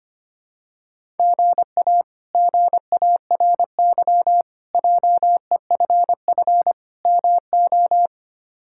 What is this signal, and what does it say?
Morse code, heard as a keyed tone, about 25 wpm, 700 Hz. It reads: GA GARY JEFF MO
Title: K1USN SST QSO Practice - 25wpm